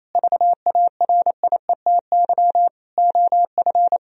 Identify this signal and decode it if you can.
Morse code, heard as a keyed tone, about 27 wpm, 700 Hz. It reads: VARIETY OF
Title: Sentences from Top 1000 Words - Review Part 4 - 28wpm